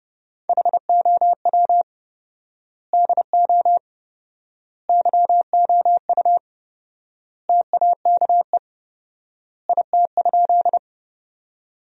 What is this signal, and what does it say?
Morse code, heard as a keyed tone, about 32 wpm, 700 Hz. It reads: HOW DO YOU TAKE IT?
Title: 4x Word-Spacing - Sentences from Top 200 Words - 30wpm